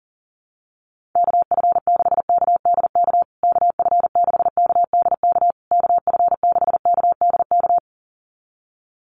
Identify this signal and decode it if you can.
Morse code, heard as a keyed tone, about 40 wpm, 700 Hz. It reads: KF6KDK KF6KDK KF6KDK
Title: USA Call Signs Repeated 3x 40wpm